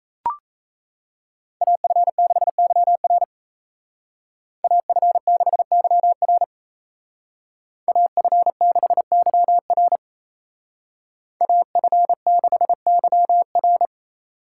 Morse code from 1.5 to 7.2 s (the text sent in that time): AF6YR AF6YR